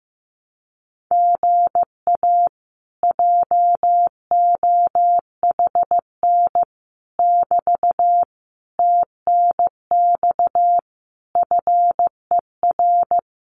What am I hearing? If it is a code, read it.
Morse code, heard as a keyed tone, about 15 wpm, 700 Hz. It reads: GA JOHN <BT> TNX FER
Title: Common QSO Phrases Encore 15wpm